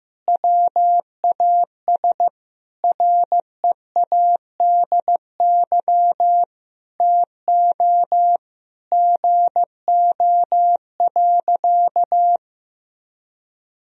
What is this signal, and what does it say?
Morse code, heard as a keyed tone, about 15 wpm, 700 Hz. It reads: WAS READY TO GO.
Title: Sentences from Top 1000 Words - Review Part 2 - 15wpm